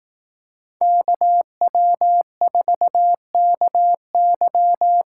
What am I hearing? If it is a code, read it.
Morse code, heard as a keyed tone, about 19 wpm, 700 Hz. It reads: KW4KY